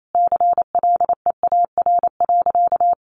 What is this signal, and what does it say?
Morse code, heard as a keyed tone, about 28 wpm, 700 Hz. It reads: CLEAR.
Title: Sentences from Top 1400 Words 28wpm